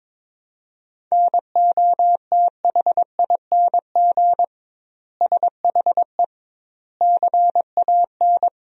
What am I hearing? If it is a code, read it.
Morse code, heard as a keyed tone, about 22 wpm, 700 Hz. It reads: NOTHING SHE CAN
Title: 2x Word-Spacing - Sentences from Top 500 Words - 22wpm